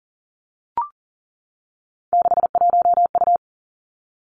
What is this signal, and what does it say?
Morse code, heard as a keyed tone, about 42 wpm, 700 Hz. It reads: 61U
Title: Three Challenging Letter-Number Combinations 40wpm